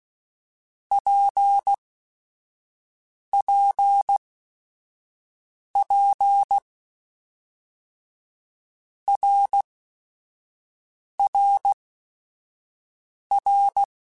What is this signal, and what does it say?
Morse code, heard as a keyed tone, about 16 wpm, spaced as for 5 wpm, 800 Hz. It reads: PPP RRR
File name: Basic Sets / Basic Letter Groups Collective7 / RKPX@5